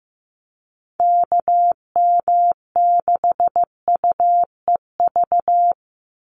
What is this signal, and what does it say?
Morse code, heard as a keyed tone, about 15 wpm, 700 Hz. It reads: KM6UEV
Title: USA Call Signs 15wpm